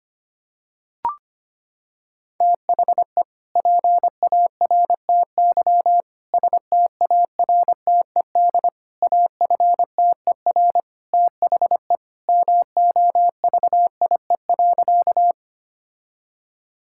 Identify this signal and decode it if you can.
Morse code, heard as a keyed tone, about 25 wpm, 690 Hz. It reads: THE PARTY STARTED AFTER THE MOVIE.
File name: practice / 25wpm / Sentences with 2-Word Phrases - 25wpm